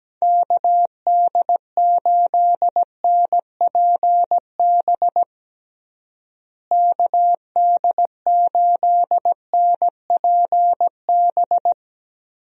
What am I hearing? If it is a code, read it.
Morse code, heard as a keyed tone, about 17 wpm, 700 Hz. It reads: KD8NPB KD8NPB
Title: USA Call Signs - 6 Characters - Repeated 3x - 17wpm